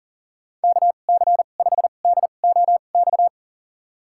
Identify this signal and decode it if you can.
Morse code, heard as a keyed tone, about 40 wpm, spaced as for 29 wpm, 700 Hz. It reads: KC5DOX
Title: USA Call Signs - 2x Character Spacing - 40wpm